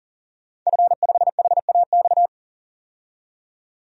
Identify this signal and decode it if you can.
Morse code, heard as a keyed tone, about 40 wpm, 700 Hz. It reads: F5HAX
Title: DXCC-100 Call Signs - 5 Characters - 40wpm